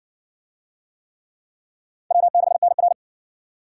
Morse code, heard as a keyed tone, about 60 wpm, 700 Hz. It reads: W6NR